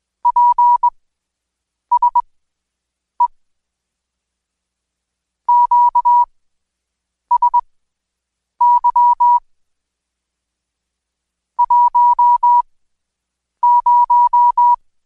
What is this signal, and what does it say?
Morse code, heard as a keyed tone, about 21 wpm, spaced as for 7 wpm, 970 Hz. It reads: PSE QSY 10
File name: Code Quick / AudioDownld2 / Seg113-Day18-DayDrill 18